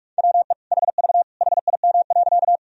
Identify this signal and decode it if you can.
Morse code, heard as a keyed tone, about 46 wpm, 700 Hz. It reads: WE HV HIM.
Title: Sentences from Top 100 Words and Abbreviations 45wpm